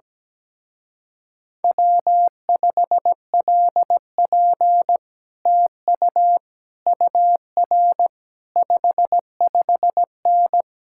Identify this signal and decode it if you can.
Morse code, heard as a keyed tone, about 17 wpm, 700 Hz. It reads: W5LP TU UR 55N
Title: Speed-Racing Common QSO Phrases Encore - 17wpm